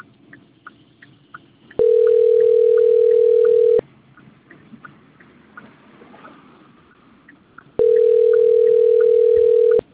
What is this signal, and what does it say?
Ringback tone (North America), 440 + 480 Hz.